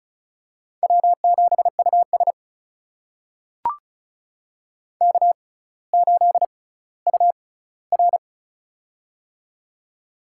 Morse code, heard as a keyed tone, about 36 wpm, spaced as for 18 wpm, 700 Hz. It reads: W7US K8UR
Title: USA Call Signs - 4 Characters - 6x Character Spacing - 35wpm